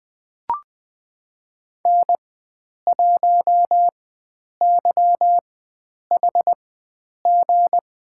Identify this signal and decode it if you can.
Morse code, heard as a keyed tone, about 20 wpm, spaced as for 9 wpm, 700 Hz. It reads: N1YHG
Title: USA Call Signs - 5 Characters - 4x Character Spacing - 20wpm